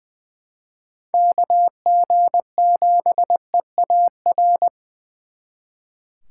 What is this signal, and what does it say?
Morse code, heard as a keyed tone, about 24 wpm, 700 Hz. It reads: KG7EAR
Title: Speed Racing - USA 2x3 Call Signs 20wpm